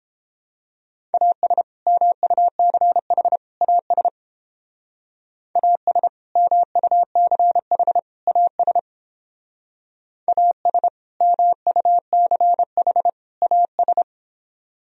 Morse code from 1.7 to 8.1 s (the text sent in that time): MUCH AS AS MUCH